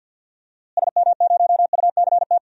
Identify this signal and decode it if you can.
Morse code, heard as a keyed tone, about 47 wpm, 700 Hz. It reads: SM0UCT
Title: DXCC-100 Call Signs - 6 Characters - Repeated 3x - 50wpm